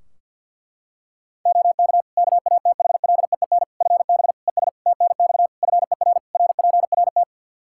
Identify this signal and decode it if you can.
Morse code, heard as a keyed tone, about 52 wpm, 700 Hz. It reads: OK KATHLEEN FB ES TNX FER RPRT
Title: Practice QSO - Exchange 2 - Info - 50wpm